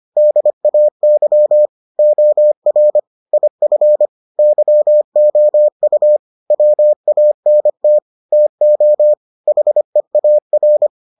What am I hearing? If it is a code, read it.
Morse code, heard as a keyed tone, about 25 wpm, 600 Hz. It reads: DAY OR IF YOU WANT TO HEAR